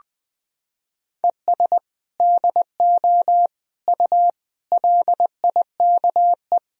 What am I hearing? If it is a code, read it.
Morse code, heard as a keyed tone, about 21 wpm, 700 Hz. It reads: ES DO U LIKE
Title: Sentences from Top 100 Words and Abbreviations 20wpm